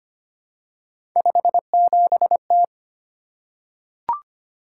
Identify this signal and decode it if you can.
Morse code, heard as a keyed tone, about 25 wpm, 700 Hz. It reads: 57T
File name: Three Letter-Number Combinations - 25wpm